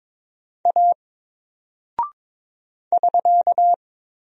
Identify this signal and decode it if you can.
Morse code, heard as a keyed tone, about 22 wpm, 700 Hz. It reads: A <SK>
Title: Single Character (Letter:Number:Punctuation:Prosign) 22wpm